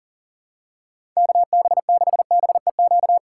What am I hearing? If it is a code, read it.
Morse code, heard as a keyed tone, about 40 wpm, 700 Hz. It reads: KB6BEQ
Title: USA Call Signs - 6 Characters - 2x Character Spacing - 40wpm